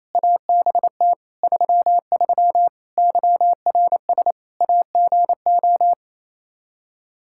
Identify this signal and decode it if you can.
Morse code, heard as a keyed tone, about 28 wpm, 700 Hz. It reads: ABT 33 YRS AGO
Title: K1USN SST QSO Practice 28wpm